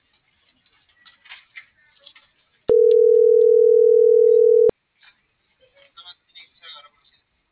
Ringback tone (North America), 440 + 480 Hz.
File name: CENTRAL MORENA / out ram.lmorena.2025